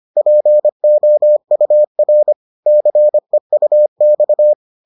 Morse code, heard as a keyed tone, about 25 wpm, 600 Hz. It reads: POUR CEUX
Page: bulletin UFT sous forme texte et en CW